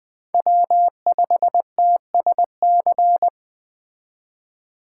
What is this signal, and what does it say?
Morse code, heard as a keyed tone, about 20 wpm, 700 Hz. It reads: W5TSC
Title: Speed Racing - USA 1x3 Call Signs 17wpm